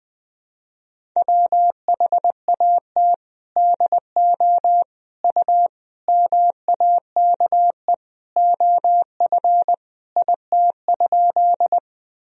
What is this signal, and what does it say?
Morse code, heard as a keyed tone, about 20 wpm, 700 Hz. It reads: WHAT DO U MAKE OF IT?
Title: Sentences from Top 100 Words and Abbreviations 20wpm